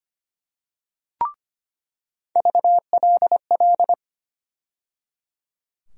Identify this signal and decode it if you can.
Morse code, heard as a keyed tone, about 25 wpm, 700 Hz. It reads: VLL
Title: USA Call Sign Postfixes - 3 Letters 25wpm